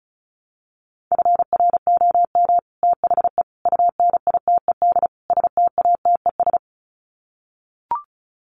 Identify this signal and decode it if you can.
Morse code, heard as a keyed tone, about 34 wpm, 700 Hz. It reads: FROM THE UNITED STATES
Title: Sets of 4 Words 35wpm